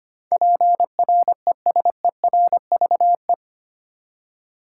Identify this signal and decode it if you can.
Morse code, heard as a keyed tone, about 25 wpm, 700 Hz. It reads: PRESERVE
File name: Eight Letter Words - 25wpm